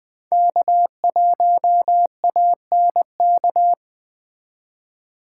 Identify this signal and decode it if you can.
Morse code, heard as a keyed tone, about 21 wpm, 700 Hz. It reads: K1ANK
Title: Speed Racing - USA 1x3 Call Signs 17wpm